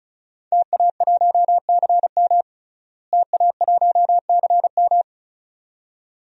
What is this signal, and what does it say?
Morse code, heard as a keyed tone, about 34 wpm, 700 Hz. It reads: TA1CM TA1CM
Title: DXCC-100 Call Signs - 5 Characters - Repeated 2x - 35wpm